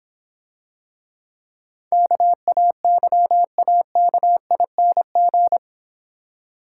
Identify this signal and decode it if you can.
Morse code, heard as a keyed tone, about 25 wpm, 700 Hz. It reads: KAYAKING
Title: Difficult Words - Speed-Racing with Hints - 20wpm